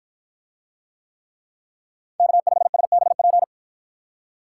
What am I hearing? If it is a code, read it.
Morse code, heard as a keyed tone, about 53 wpm, 700 Hz. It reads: K5SBP